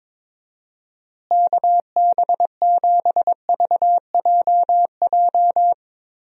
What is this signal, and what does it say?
Morse code, heard as a keyed tone, about 22 wpm, 700 Hz. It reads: KB7VJJ
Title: USA Call Signs - 6 Characters - Repeated 3x - 22wpm